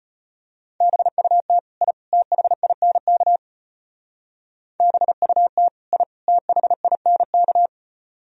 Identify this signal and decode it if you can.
Morse code, heard as a keyed tone, about 31 wpm, 700 Hz. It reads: BUT I THINK BUT I THINK
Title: Speed-Racing - Sets of 3 Words 25wpm